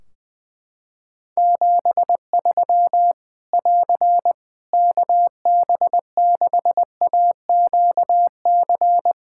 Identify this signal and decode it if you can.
Morse code, heard as a keyed tone, about 20 wpm, 700 Hz. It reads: 73 <AR> KB6AQC
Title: Practice QSO - Exchange 4 - Ending - 20wpm